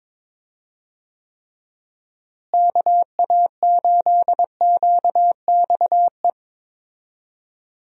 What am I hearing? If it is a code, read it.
Morse code, heard as a keyed tone, about 22 wpm, 700 Hz. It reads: KA8QXE
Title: USA Call Signs - 6 Characters - Repeated 3x - 22wpm